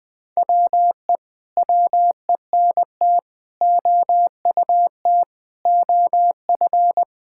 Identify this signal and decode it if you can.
Morse code, heard as a keyed tone, about 20 wpm, 700 Hz. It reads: WE WENT OUT OF
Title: Sentences with 9-Word Phrases - Spoken Beginning - 20wpm